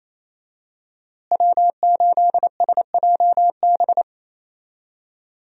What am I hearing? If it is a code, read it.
Morse code, heard as a keyed tone, about 27 wpm, 700 Hz. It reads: W8SJB
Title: USA Call Signs - 5 Characters - Repeated 3x - 28wpm